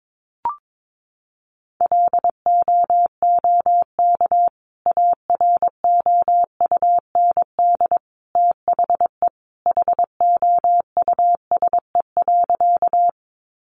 Morse code, heard as a keyed tone, about 22 wpm, 700 Hz. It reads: LOOK AROUND THE HOUSE.
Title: Sentences from Top 600 Words 22wpm